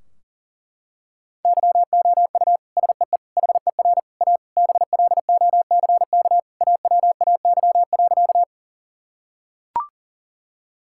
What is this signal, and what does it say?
Morse code, heard as a keyed tone, about 40 wpm, 700 Hz. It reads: YOU SEE HER A BLOCK AWAY.
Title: Sentences from Top 1600 Words 40wpm